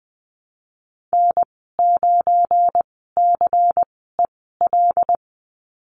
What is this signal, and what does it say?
Morse code, heard as a keyed tone, about 20 wpm, spaced as for 15 wpm, 700 Hz. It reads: N9CEL
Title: 179 Lesson 29g - Slash - Call Signs 20wpm